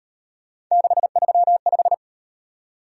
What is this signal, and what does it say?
Morse code, heard as a keyed tone, about 38 wpm, 700 Hz. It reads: 635